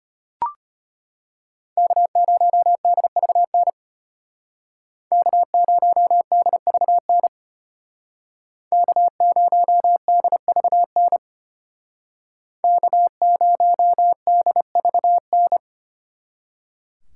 Morse code, heard as a keyed tone, about 31 wpm, 700 Hz. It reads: K0DVN K0DVN K0DVN K0DVN
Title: Speed Racing - USA Call Signs 25wpm